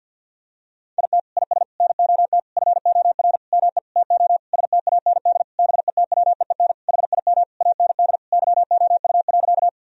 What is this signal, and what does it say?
Morse code, heard as a keyed tone, about 50 wpm, 700 Hz. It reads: IT IS NOT FOR ME TO STAND BETWEEN HIM AND YOU.
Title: Sentences from Top 1000 Words - Review Part 1 - 50wpm